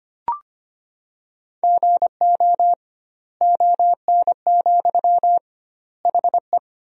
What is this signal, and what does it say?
Morse code, heard as a keyed tone, about 25 wpm, 700 Hz. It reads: GO ON, HE
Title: 2x Word-Spacing - Sentences from Top 100 Words - 25wpm